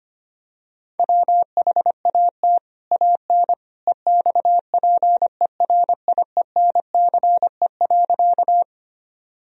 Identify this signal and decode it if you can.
Morse code, heard as a keyed tone, about 25 wpm, 700 Hz. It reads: WHAT AN EXPERIENCE.
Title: Sentences from Top 1000 Words - Review Part 3 - 25wpm